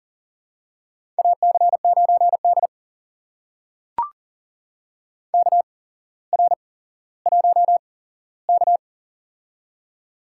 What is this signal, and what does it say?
Morse code, heard as a keyed tone, about 39 wpm, spaced as for 17 wpm, 700 Hz. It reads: AC9D KR1K
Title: USA Call Signs - 4 Characters - 8x Character Spacing - 40wpm